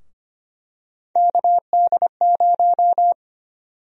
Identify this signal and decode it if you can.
Morse code, heard as a keyed tone, about 25 wpm, 700 Hz. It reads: KD0
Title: USA Call Sign Prefixes - 25wpm